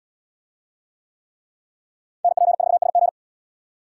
Morse code, heard as a keyed tone, about 70 wpm, 700 Hz. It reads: NP4SG